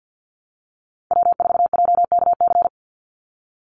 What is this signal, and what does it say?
Morse code, heard as a keyed tone, about 50 wpm, 700 Hz. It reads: W4JKC
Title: NA SOTA Chaser Callsigns 50wpm